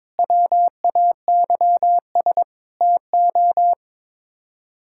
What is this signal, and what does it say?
Morse code, heard as a keyed tone, about 22 wpm, 700 Hz. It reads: WAYS TO
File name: practice / 22wpm / Sets of 7 Words - 22wpm